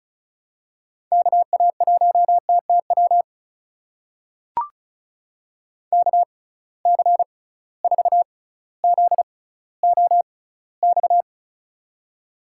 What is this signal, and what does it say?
Morse code, heard as a keyed tone, about 36 wpm, spaced as for 18 wpm, 700 Hz. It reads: KA1TTW KC4ZOX
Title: USA Call Signs - 6 Characters - 6x Character Spacing - 35wpm